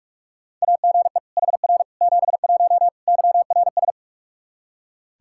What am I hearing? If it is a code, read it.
Morse code, heard as a keyed tone, about 41 wpm, 700 Hz. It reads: AGE HR 71 YRS
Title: Practice QSO - Exchange 3 - More Info - 45wpm